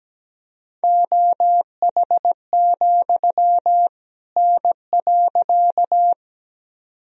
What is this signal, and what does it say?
Morse code, heard as a keyed tone, about 17 wpm, 700 Hz. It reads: OH, N.
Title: Sentences from Top 500 Words and Abbreviations 17wpm